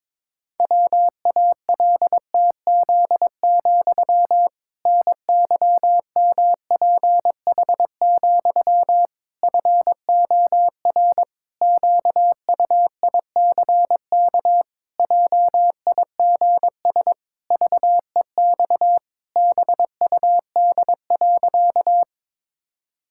Morse code, heard as a keyed tone, about 22 wpm, 700 Hz. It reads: WALTZ, NYMPH, FOR QUICK JIGS VEX BUD.
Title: Pangram Challenge v2 - 22wpm